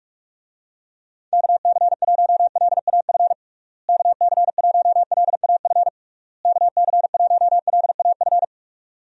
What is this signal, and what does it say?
Morse code, heard as a keyed tone, about 48 wpm, 700 Hz. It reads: KC1LAF KC1LAF KC1LAF
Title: USA Call Signs - 6 Characters - Repeated 3x - 45wpm